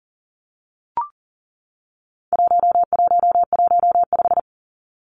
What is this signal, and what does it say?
Morse code, heard as a keyed tone, about 40 wpm, 700 Hz. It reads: 1115
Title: Numbers - 4-Digit 40wpm